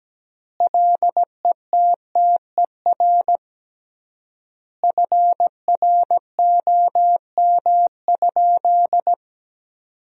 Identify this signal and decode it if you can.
Morse code, heard as a keyed tone, about 17 wpm, 700 Hz. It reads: LETTER FROM?
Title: 3x Word-Spacing - Sentences from Top 300 Words - 17wpm